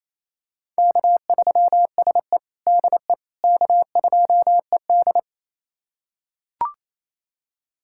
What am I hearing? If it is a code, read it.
Morse code, heard as a keyed tone, about 28 wpm, 690 Hz. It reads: K3SE DE K2ED
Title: USA Callsign DE Callsign - 4 Characters - 28wpm